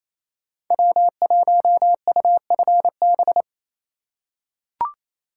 Morse code, heard as a keyed tone, about 28 wpm, 700 Hz. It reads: W1UFB
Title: USA Call Signs - 5 Characters - 2x Character Spacing - 28wpm